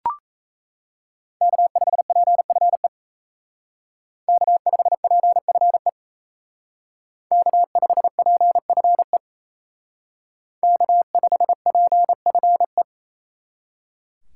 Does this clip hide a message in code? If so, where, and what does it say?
Yes: Morse code from 1.4 to 12.8 s, heard as a keyed tone, about 35 wpm, 700 Hz. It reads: K5PFE K5PFE K5PFE K5PFE